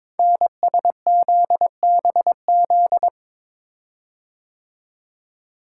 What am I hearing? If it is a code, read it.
Morse code, heard as a keyed tone, about 21 wpm, 700 Hz. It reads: NSZBZ
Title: Code Groups - 5 Letters - 22wpm